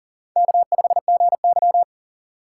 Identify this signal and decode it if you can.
Morse code, heard as a keyed tone, about 51 wpm, 700 Hz. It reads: K5GY